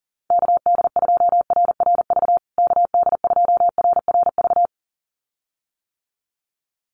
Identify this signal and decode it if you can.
Morse code, heard as a keyed tone, about 40 wpm, 700 Hz. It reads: KD2RRV KD2RRV
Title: USA Call Signs Repeated 2x 40wpm